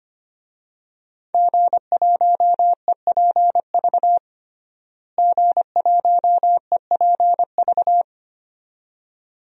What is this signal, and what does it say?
Morse code, heard as a keyed tone, about 25 wpm, 700 Hz. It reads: G1EPV G1EPV
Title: DXCC-100 Call Signs - 5 Characters - Repeated 2x - 25wpm